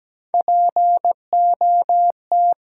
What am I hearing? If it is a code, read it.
Morse code, heard as a keyed tone, about 17 wpm, 700 Hz. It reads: POT
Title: Sentences from Top 1000 Words - New Word Drill - 17wpm